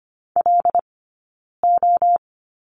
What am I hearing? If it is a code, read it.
Morse code, heard as a keyed tone, about 25 wpm, 700 Hz. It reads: L O
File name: course / 25:10wpm / 092 Lesson 16d - Letter c - Two Characters 25wpm